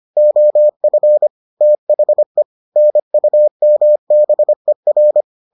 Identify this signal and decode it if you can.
Morse code, heard as a keyed tone, about 25 wpm, 600 Hz. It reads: OF THE NUMBER